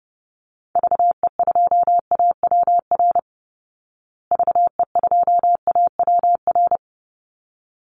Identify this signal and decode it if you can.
Morse code, heard as a keyed tone, about 30 wpm, 700 Hz. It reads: VE2AWR VE2AWR
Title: Call Signs Repeated 3x - 30wpm